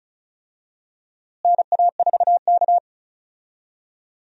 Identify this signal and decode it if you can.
Morse code, heard as a keyed tone, about 34 wpm, 700 Hz. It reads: NA4K
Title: USA Call Signs - 4 Characters - Repeated 2x - 35wpm